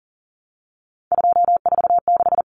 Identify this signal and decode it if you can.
Morse code, heard as a keyed tone, about 40 wpm, 700 Hz. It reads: 246
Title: Numbers - 3-Digit 40wpm